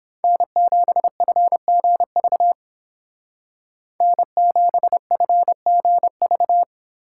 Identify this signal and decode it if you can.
Morse code, heard as a keyed tone, about 28 wpm, 700 Hz. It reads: N7FGV N7FGV